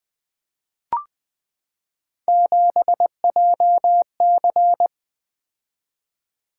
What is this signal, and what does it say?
Morse code, heard as a keyed tone, about 20 wpm, 700 Hz. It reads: 7JC
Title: Three Letter-Number Combinations - 20wpm